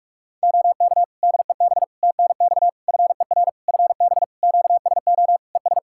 Morse code, heard as a keyed tone, about 42 wpm, 700 Hz. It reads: OK DEB TNX FER FB QSO ES
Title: Practice QSO - Exchange 4 - Ending - 45wpm